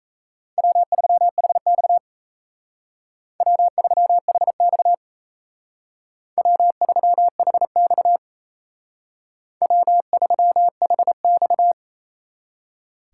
Morse code, heard as a keyed tone, about 33 wpm, 700 Hz. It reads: W3HX W3HX W3HX W3HX
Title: Speed Racing - USA 1x2 Call Signs 28wpm